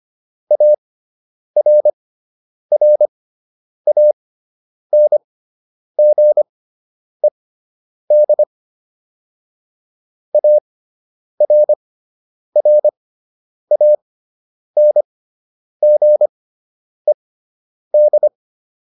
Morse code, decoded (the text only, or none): ARRANGED ARRANGED